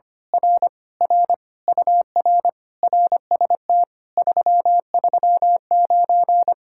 Morse code, heard as a keyed tone, about 24 wpm, 700 Hz. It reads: R R UR RST 339
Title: Speed-Racing Common QSO Phrases - 25wpm